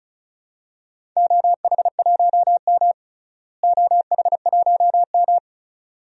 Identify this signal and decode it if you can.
Morse code, heard as a keyed tone, about 36 wpm, 690 Hz. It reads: OH1M OH1M
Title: DXCC-100 Call Signs - 4 Characters - Repeated 2x - 35wpm